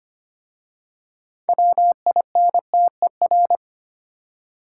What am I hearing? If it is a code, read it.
Morse code, heard as a keyed tone, about 25 wpm, 700 Hz. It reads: WINTER
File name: practice / 25wpm / Easy Words with Hints - 25wpm